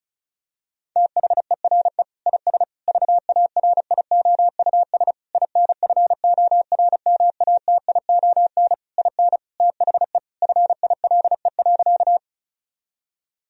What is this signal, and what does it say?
Morse code, heard as a keyed tone, about 36 wpm, 700 Hz. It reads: THERE IS VARIOUS INFORMATION IN THE FILE.
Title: Sentences from Top 1000 Words - Review Part 4 - 35wpm